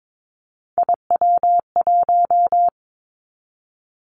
Morse code, heard as a keyed tone, about 22 wpm, 700 Hz. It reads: IW1
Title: Callsign Prefixes for Top 50 DXCC - 22wpm